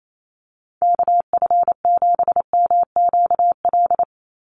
Morse code, heard as a keyed tone, about 28 wpm, 700 Hz. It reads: KF7MQL
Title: USA Call Signs 28wpm